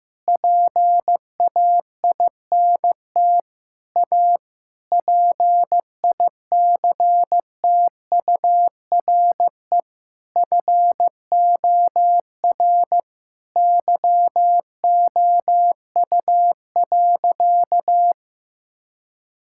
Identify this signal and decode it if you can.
Morse code, heard as a keyed tone, about 15 wpm, 700 Hz. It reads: PAINT A PICTURE FOR YOU.
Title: Sentences from Top 1000 Words - Review Part 6 - 15wpm